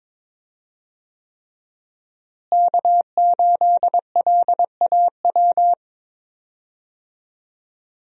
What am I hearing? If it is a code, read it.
Morse code, heard as a keyed tone, about 22 wpm, 700 Hz. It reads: K8LAW
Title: USA Call Signs - 5 Characters - Repeated 3x - 22wpm